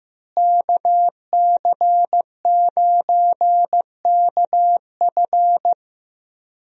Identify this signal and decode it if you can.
Morse code, heard as a keyed tone, about 15 wpm, 700 Hz. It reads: KC9KF
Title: USA Call Signs - 5 Characters - Repeated 2x - 15wpm